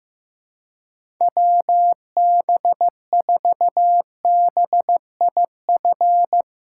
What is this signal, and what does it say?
Morse code, heard as a keyed tone, about 15 wpm, 700 Hz. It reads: WB4BIF
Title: USA Call Signs - 6 Characters - 6x Character Spacing - 15wpm